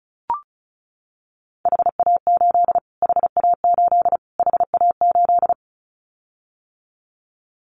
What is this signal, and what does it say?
Morse code, heard as a keyed tone, about 34 wpm, 700 Hz. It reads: HA8 HA8 HA8
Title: Callsign Prefixes for Top 25 DXCC - Repeated 3x - 35wpm